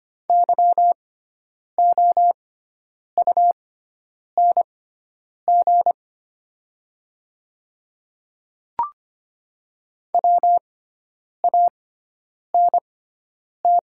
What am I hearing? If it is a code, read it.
Morse code, heard as a keyed tone, about 25 wpm, spaced as for 9 wpm, 700 Hz. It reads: YOUNG WANT
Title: 6x Character Spacing - Top 500 Words - 25wpm